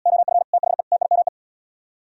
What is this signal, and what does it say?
Morse code, heard as a keyed tone, about 73 wpm, 700 Hz. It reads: OF THE TIME